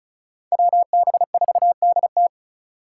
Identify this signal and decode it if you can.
Morse code, heard as a keyed tone, about 36 wpm, 700 Hz. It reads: WB4DT
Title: USA Call Signs - 3x Character Spacing - 35wpm